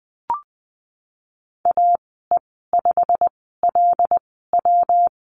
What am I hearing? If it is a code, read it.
Morse code, heard as a keyed tone, about 19 wpm, spaced as for 14 wpm, 700 Hz. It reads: AE5LW
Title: 154 Lesson 25g - Letter p - Call Signs 20wpm